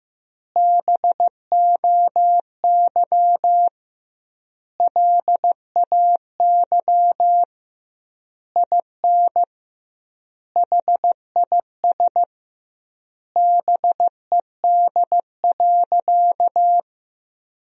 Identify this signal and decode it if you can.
Morse code, heard as a keyed tone, about 15 wpm, 700 Hz. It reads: BOY LAY IN HIS BED.
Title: 2x Word-Spacing - Sentences from Top 500 Words - 15wpm